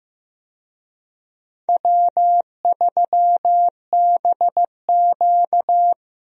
Morse code, heard as a keyed tone, about 15 wpm, 700 Hz. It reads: W3BQ
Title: USA Call Signs - 4 Characters - Repeated 2x - 15wpm